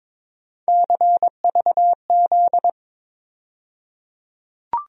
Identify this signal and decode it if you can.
Morse code, heard as a keyed tone, about 22 wpm, 700 Hz. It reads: CVZ
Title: USA Call Sign Postfixes - 3 Letters 22wpm